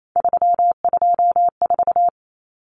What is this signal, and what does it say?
Morse code, heard as a keyed tone, about 28 wpm, 700 Hz. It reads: 324
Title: Numbers - 3-Digit 28wpm